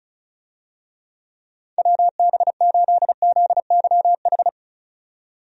Morse code, heard as a keyed tone, about 36 wpm, 700 Hz. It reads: WB8ZYH
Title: USA Call Signs - 6 Characters - 35wpm